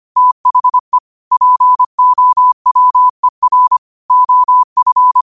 Morse code, heard as a keyed tone, about 25 wpm, 1000 Hz. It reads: THE POWER OF
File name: qotdfiles / 25wpm